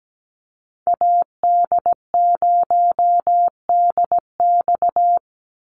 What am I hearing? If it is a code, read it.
Morse code, heard as a keyed tone, about 17 wpm, 700 Hz. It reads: AD0DX
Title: Call Signs - 17wpm